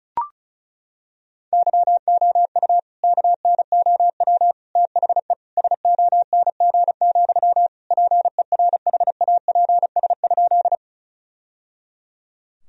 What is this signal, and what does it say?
Morse code, heard as a keyed tone, about 34 wpm, 700 Hz. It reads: YOU KNOW THE SONG, PERHAPS?
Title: Sentences from Top 1000 Words - Review Part 2 - 35wpm